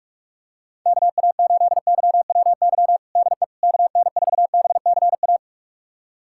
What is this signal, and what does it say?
Morse code, heard as a keyed tone, about 46 wpm, 700 Hz. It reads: KA8YWY DE KN4BCA
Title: USA Callsign DE Callsign - 6 Characters - 45wpm